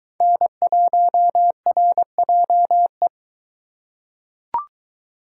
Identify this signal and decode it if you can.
Morse code, heard as a keyed tone, about 22 wpm, 700 Hz. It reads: N1RJE